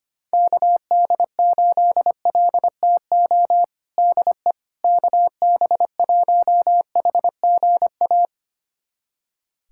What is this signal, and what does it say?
Morse code, heard as a keyed tone, about 25 wpm, 700 Hz. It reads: KD8LTO DE KB1HGA
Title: USA Callsign DE Callsign - 6 Characters - 25wpm